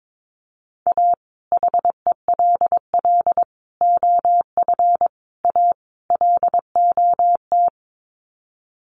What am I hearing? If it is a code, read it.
Morse code, heard as a keyed tone, about 22 wpm, 700 Hz. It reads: A HELL OF A LOT
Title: Sets of 5 Words Encore 22wpm